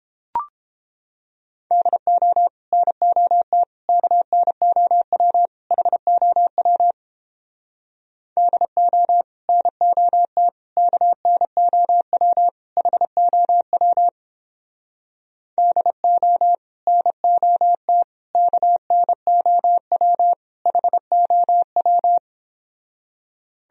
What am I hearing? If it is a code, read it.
Morse code, heard as a keyed tone, about 31 wpm, 700 Hz. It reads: DO NOT KNOW HOW DO NOT KNOW HOW DO NOT KNOW HOW